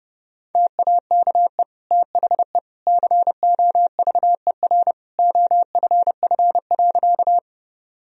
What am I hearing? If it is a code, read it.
Morse code, heard as a keyed tone, about 30 wpm, 700 Hz. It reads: TAKE THE COVER OFF.
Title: Sentences from Top 1000 Words - Review Part 1 - 30wpm